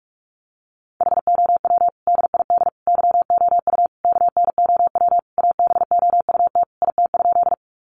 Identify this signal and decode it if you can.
Morse code, heard as a keyed tone, about 44 wpm, 700 Hz. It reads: HOW DID YOU KNOW ABOUT IT?
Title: Sentences from Top 100 Words 45wpm